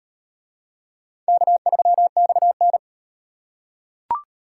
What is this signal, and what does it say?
Morse code, heard as a keyed tone, about 38 wpm, 700 Hz. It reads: K3XN